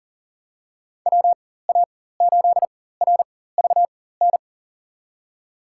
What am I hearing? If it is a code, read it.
Morse code, heard as a keyed tone, about 40 wpm, spaced as for 19 wpm, 700 Hz. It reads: WA8RVN
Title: USA Call Signs - 6 Characters - 4x Character Spacing - 40wpm